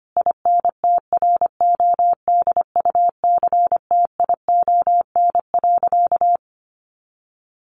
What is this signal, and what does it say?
Morse code, heard as a keyed tone, about 25 wpm, 700 Hz. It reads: INTRODUCTION.
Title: Sentences from Top 1000 Words 25wpm